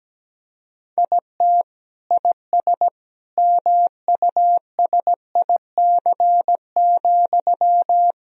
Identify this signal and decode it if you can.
Morse code, heard as a keyed tone, about 17 wpm, 700 Hz. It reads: IT IS MUSIC,
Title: Sentences from Top 1000 Words - Review Part 1 - 17wpm